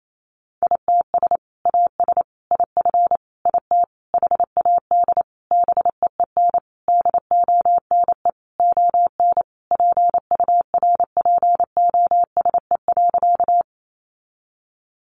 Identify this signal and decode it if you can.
Morse code, heard as a keyed tone, about 28 wpm, 700 Hz. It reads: ITS AS IF IT HAD BEEN DONE ON PURPOSE.
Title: Sentences from Top 800 Words 28wpm